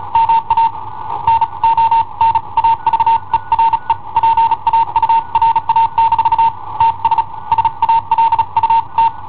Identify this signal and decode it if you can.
Morse code, heard as a keyed tone, about 35 wpm, spaced as for 30 wpm, 910 Hz. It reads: MA NON AVERE PAURA- TS SALUT
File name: i7ohp / cassetta i7ohp B